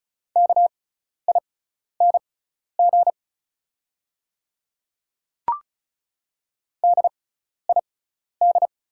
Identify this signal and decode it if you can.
Morse code, heard as a keyed tone, about 36 wpm, spaced as for 12 wpm, 700 Hz. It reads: KING DID
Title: 6x Character Spacing - Top 500 Words - 35wpm